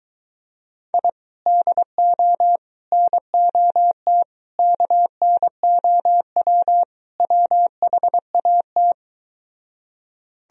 Morse code, heard as a keyed tone, about 23 wpm, 700 Hz. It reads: I DO NOT KNOW WHAT